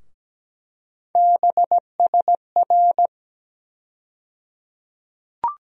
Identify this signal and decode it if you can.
Morse code, heard as a keyed tone, about 17 wpm, 700 Hz. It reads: BSR
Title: USA Call Sign Postfixes - 3 Letters 17wpm